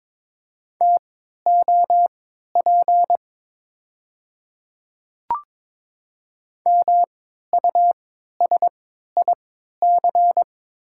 Morse code, heard as a keyed tone, about 22 wpm, spaced as for 13 wpm, 700 Hz. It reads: TOP MUSIC
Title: 3x Character Spacing - Top 500 Words - 22wpm